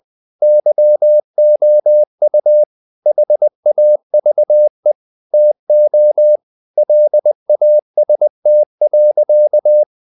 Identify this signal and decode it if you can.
Morse code, heard as a keyed tone, about 20 wpm, 600 Hz. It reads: YOU HAVE TO LAST.
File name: the old man and the sea part 2 20wpm